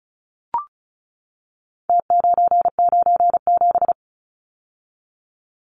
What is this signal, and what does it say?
Morse code, heard as a keyed tone, about 34 wpm, 700 Hz. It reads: T997
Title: Numbers - Contest Style - 4-Digit 35wpm